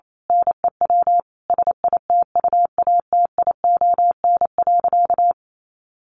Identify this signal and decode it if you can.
Morse code, heard as a keyed tone, about 28 wpm, 700 Hz. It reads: NEW SITUATION.
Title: Sentences from Top 1200 Words 28wpm